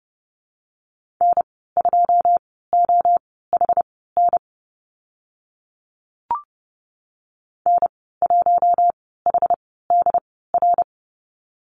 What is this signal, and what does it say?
Morse code, heard as a keyed tone, about 30 wpm, spaced as for 17 wpm, 700 Hz. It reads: N2OHN N1HDR
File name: Lesson 14g - Number 2 - Call Signs 30wpm